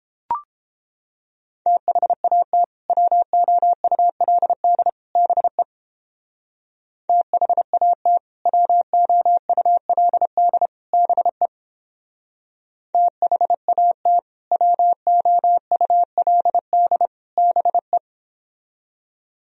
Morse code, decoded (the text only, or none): THAT WOULD BE THAT WOULD BE THAT WOULD BE